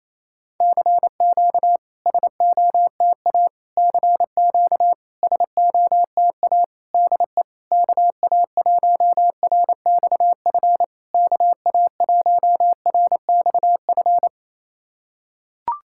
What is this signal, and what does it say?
Morse code, heard as a keyed tone, about 28 wpm, 700 Hz. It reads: CQ SOTA CQ SOTA DE KA1RXF KA1RXF
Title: SOTA QSO Practice 28wpm